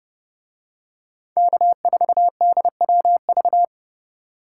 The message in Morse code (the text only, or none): K4DWV